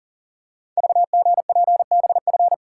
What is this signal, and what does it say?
Morse code, heard as a keyed tone, about 39 wpm, 700 Hz. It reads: VGPBF
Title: Code Groups - 5 Letters - 40wpm